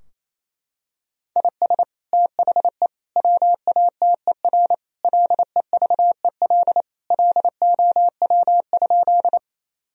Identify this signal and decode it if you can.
Morse code, heard as a keyed tone, about 28 wpm, 700 Hz. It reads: IS THE WATER LEVEL LOW?
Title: Sentences from Top 1000 Words - Review Part 4 - 28wpm